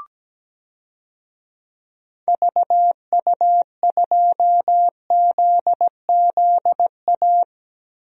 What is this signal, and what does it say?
Morse code, heard as a keyed tone, about 17 wpm, 700 Hz. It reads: VU2ZZA
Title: DXCC-100 Call Signs - 6 Characters - Repeated 3x - 17wpm